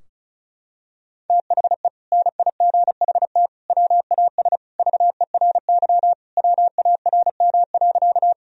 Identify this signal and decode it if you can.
Morse code, heard as a keyed tone, about 36 wpm, 700 Hz. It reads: THE NIGHT WAS VERY WARM.
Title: Sentences from Top 1000 Words - Review Part 2 - 35wpm